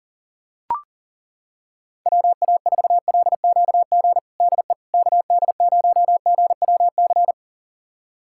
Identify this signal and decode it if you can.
Morse code, heard as a keyed tone, about 40 wpm, 700 Hz. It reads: WA4LQG DE KD0GWC
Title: USA Callsign DE Callsign - 6 Characters - 40wpm